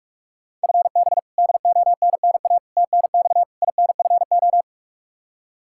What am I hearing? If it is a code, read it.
Morse code, heard as a keyed tone, about 41 wpm, 700 Hz. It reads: FB DONNA TNX INFO
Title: Practice QSO - Exchange 3 - More Info - 45wpm